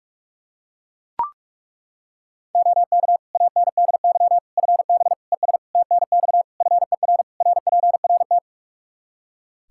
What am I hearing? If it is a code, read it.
Morse code, heard as a keyed tone, about 41 wpm, 700 Hz. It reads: OK ANDY FB ES TNX FER RPRT
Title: Practice QSO - Exchange 2 - Info - 45wpm